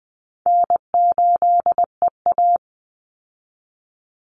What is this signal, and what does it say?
Morse code, heard as a keyed tone, about 20 wpm, 700 Hz. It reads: N8EA
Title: Call Signs Repeated 2x - 20wpm